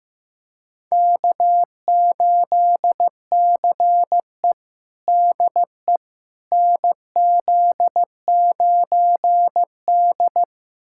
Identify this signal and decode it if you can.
Morse code, heard as a keyed tone, about 15 wpm, 700 Hz. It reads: K8CE DE NZ9D
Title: USA Callsign DE Callsign - 4 Characters - 15wpm